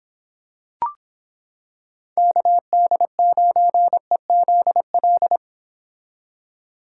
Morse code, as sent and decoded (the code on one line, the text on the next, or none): -.- -.. ----. --.. .-..
KD9EZL